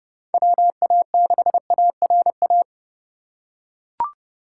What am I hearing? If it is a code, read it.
Morse code, heard as a keyed tone, about 30 wpm, 700 Hz. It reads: WA6ARA